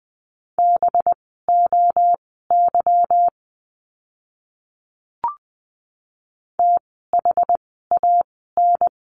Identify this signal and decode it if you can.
Morse code, heard as a keyed tone, about 20 wpm, spaced as for 15 wpm, 700 Hz. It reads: BOY THAN
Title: 185 Lesson 30f - Letter b - Words 20wpm